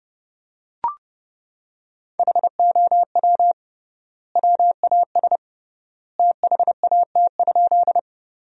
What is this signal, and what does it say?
Morse code, heard as a keyed tone, about 30 wpm, 700 Hz. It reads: HOW WAS THAT?
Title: 3x Word-Spacing - Sentences from Top 100 Words - 30wpm